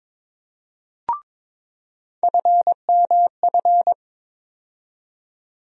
Morse code, heard as a keyed tone, about 22 wpm, 700 Hz. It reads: FMF
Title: USA Call Sign Postfixes - 3 Letters 22wpm